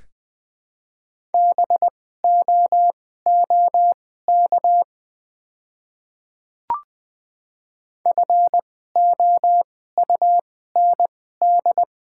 Morse code, heard as a keyed tone, about 20 wpm, spaced as for 14 wpm, 700 Hz. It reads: BOOK FOUND
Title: 2x Character Spacing - Top 500 Words - 20wpm